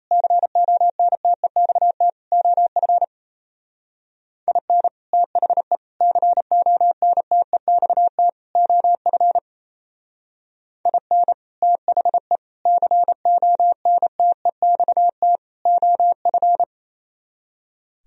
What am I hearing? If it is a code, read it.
Morse code, heard as a keyed tone, about 35 wpm, 700 Hz. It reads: CONTEXT OF IN THE CONTEXT OF IN THE CONTEXT OF
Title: Speed-Racing - Sets of 4 Words 28wpm